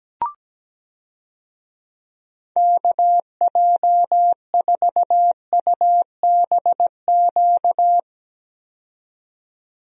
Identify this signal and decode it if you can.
Morse code, heard as a keyed tone, about 17 wpm, 700 Hz. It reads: KJ4UBQ
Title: USA Call Signs - 6 Characters - 17wpm